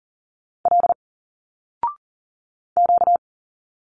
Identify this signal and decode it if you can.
Morse code, heard as a keyed tone, about 40 wpm, 700 Hz. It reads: L Q
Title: Single Letters 40wpm